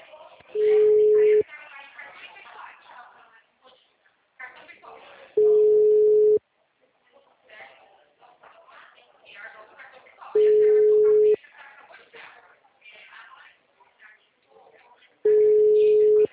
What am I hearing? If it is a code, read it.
Ringback tone (Europe, 425 Hz), with speech.
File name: out- -2010- - - .70434